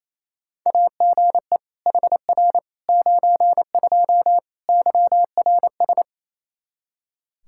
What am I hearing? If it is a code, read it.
Morse code, heard as a keyed tone, about 27 wpm, 700 Hz. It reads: AGE HR 92 YRS
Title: Practice QSO - Exchange 3 - More Info - 28wpm